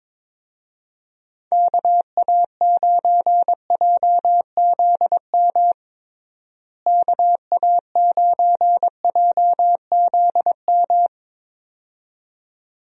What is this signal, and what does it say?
Morse code, heard as a keyed tone, about 22 wpm, 700 Hz. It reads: KA9JZM KA9JZM
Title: USA Call Signs - 6 Characters - Repeated 2x - 22wpm